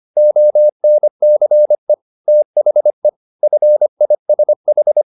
Morse code, heard as a keyed tone, about 25 wpm, 600 Hz. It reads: ONCE THE FISH
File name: the old man and the sea part 1 25wpm